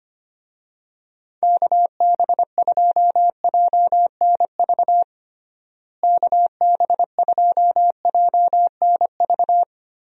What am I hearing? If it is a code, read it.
Morse code, heard as a keyed tone, about 25 wpm, 700 Hz. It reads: KB2JNV KB2JNV
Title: USA Call Signs - 6 Characters - Repeated 3x - 25wpm